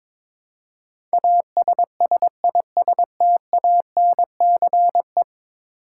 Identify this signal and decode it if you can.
Morse code, heard as a keyed tone, about 22 wpm, 700 Hz. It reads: ASSISTANCE
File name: practice / 22wpm / 10 Letter Words - First 3 Letters Given - 22wpm